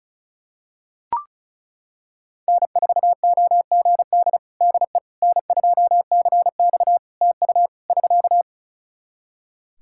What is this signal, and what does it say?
Morse code, heard as a keyed tone, about 35 wpm, 700 Hz. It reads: N4OGD DE N2CX TU <SK>
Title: Practice QSO - Exchange 4 - Ending - 35wpm